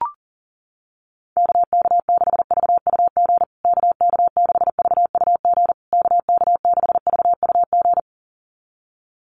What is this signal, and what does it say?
Morse code, heard as a keyed tone, about 38 wpm, 700 Hz. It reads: KK6VUG KK6VUG KK6VUG
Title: USA Call Signs Repeated 3x 40wpm